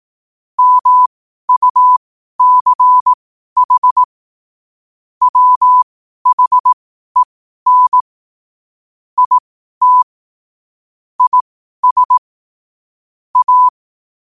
Morse code, heard as a keyed tone, about 18 wpm, spaced as for 12 wpm, 1000 Hz. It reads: MUCH WHEN IT IS A
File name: qotdfiles / 12wpm